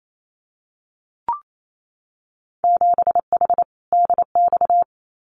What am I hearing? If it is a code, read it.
Morse code, heard as a keyed tone, about 28 wpm, 700 Hz. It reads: 7H DX
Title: Winter Field Day Exchanges Encore 28wpm